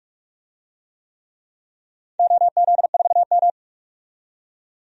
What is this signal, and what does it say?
Morse code, heard as a keyed tone, about 42 wpm, 700 Hz. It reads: OZ4M
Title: DXCC-100 Call Signs - 4 Characters - 45wpm